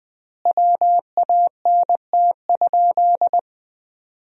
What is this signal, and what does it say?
Morse code, heard as a keyed tone, about 21 wpm, 700 Hz. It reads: WANT?
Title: 3x Word-Spacing - Sentences from Top 200 Words - 20wpm